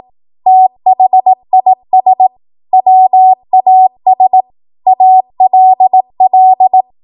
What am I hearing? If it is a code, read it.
Morse code, heard as a keyed tone, about 18 wpm, 750 Hz. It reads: THIS WAS ALL
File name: Morse / 18WPM